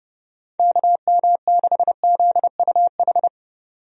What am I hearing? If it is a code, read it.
Morse code, heard as a keyed tone, about 30 wpm, 700 Hz. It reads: KM6ZUH
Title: USA Call Signs - 6 Characters - 30wpm